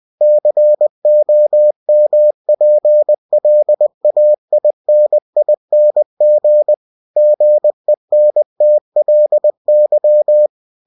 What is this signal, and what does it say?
Morse code, heard as a keyed tone, about 20 wpm, 600 Hz. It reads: COMPLAINING GENTLY
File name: cw-books / winnie-the-pooh / winnie-20wpm-page-0102